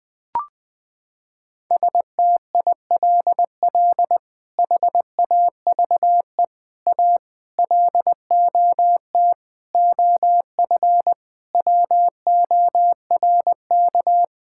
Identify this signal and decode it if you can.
Morse code, heard as a keyed tone, about 20 wpm, 700 Hz. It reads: STILL HAVE A LOT OF WORK
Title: Sets of 8 Words - 20wpm